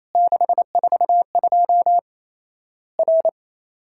Morse code, heard as a keyed tone, about 28 wpm, 700 Hz. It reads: 642 R
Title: Worked All Europe DX Contest - QTC Exchange - 28wpm